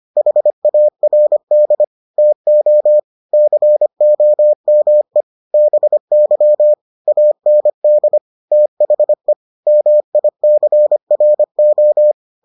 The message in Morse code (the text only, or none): HARD TO COME BY AND THE MICRO